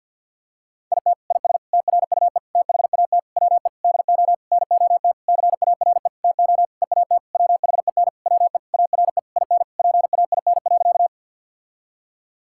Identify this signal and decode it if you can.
Morse code, heard as a keyed tone, about 50 wpm, 700 Hz. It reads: IT IS TRUE THAT WE DO NOT CARE TO EAT WHEN WE ARE IN PAIN.
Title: Sentences from Top 1600 Words 50wpm